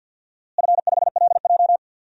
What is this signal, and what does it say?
Morse code, heard as a keyed tone, about 52 wpm, 700 Hz. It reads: F5LJ
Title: DXCC-100 Call Signs - 4 Characters - Repeated 2x - 50wpm